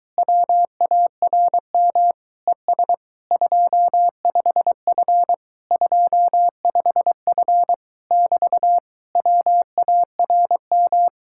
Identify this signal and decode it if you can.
Morse code, heard as a keyed tone, about 23 wpm, 700 Hz. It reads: WARM ES 25F 25F <BT> WARM